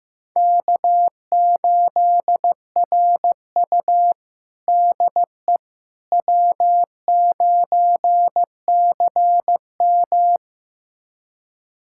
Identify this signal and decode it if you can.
Morse code, heard as a keyed tone, about 15 wpm, 700 Hz. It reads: K8RU DE W9CM
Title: USA Callsign DE Callsign - 4 Characters - 15wpm